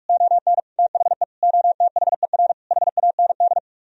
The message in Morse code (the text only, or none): ON THE OTHER HAND